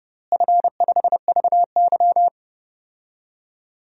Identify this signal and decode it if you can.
Morse code, heard as a keyed tone, about 30 wpm, 700 Hz. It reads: F5VY
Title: DXCC-100 Call Signs - 4 Characters - Repeated 3x - 30wpm